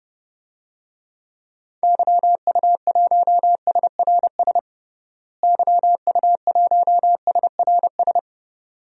Morse code, heard as a keyed tone, about 30 wpm, 700 Hz. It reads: YU1SRS YU1SRS
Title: DXCC-100 Call Signs - 6 Characters - Repeated 2x - 30wpm